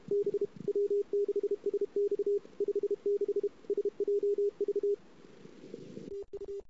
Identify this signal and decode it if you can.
Morse code, heard as a keyed tone, about 32 wpm, 410 Hz. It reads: B W6SX 56 SJV TU
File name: CW / 2008-ss-cw-k5tr-op-n5rz -22:00